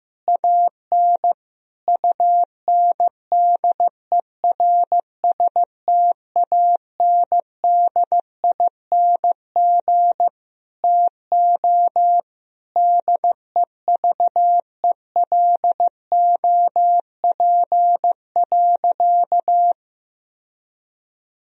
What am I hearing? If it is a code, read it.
Morse code, heard as a keyed tone, about 16 wpm, 700 Hz. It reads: AN UNDERSTANDING TO DEVELOP.
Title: Sentences from Top 1000 Words - Review Part 3 - 15wpm